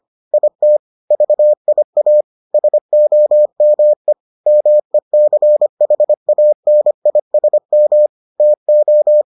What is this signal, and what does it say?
Morse code, heard as a keyed tone, about 25 wpm, 600 Hz. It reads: IT VIA SOME MECHANISM TO